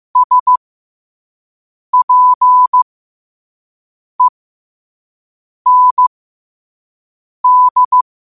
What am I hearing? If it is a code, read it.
Morse code, heard as a keyed tone, about 15 wpm, spaced as for 5 wpm, 1000 Hz. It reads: SPEND
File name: qotdfiles / 5wpm